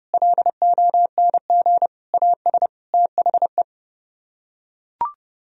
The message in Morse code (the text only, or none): LONG AS THE